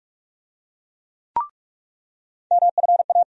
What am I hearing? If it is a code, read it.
Morse code, heard as a keyed tone, about 46 wpm, 700 Hz. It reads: MFA
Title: USA Call Sign Postfixes - 3 Letters 45wpm